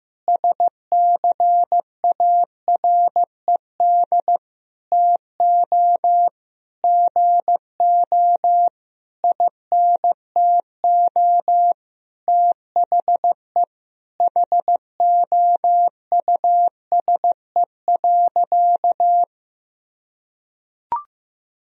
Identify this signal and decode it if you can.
Morse code, heard as a keyed tone, about 15 wpm, 700 Hz. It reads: SCARED TO GO INTO THE HOUSE.
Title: Sentences from Top 1000 Words - Review Part 4 - 15wpm